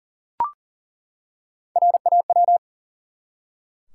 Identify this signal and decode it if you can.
Morse code, heard as a keyed tone, about 40 wpm, 700 Hz. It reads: RAW
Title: Three Letter Words - 40wpm